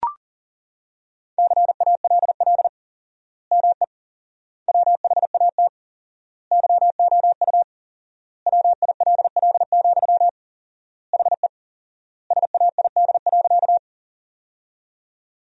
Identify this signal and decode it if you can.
Morse code, heard as a keyed tone, about 40 wpm, 700 Hz. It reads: CALL ME WHAT YOU WILL, HE SAID.
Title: 4x Word-Spacing - Sentences from Top 200 Words - 40wpm